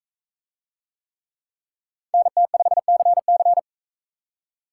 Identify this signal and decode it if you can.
Morse code, heard as a keyed tone, about 39 wpm, 700 Hz. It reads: NT5CC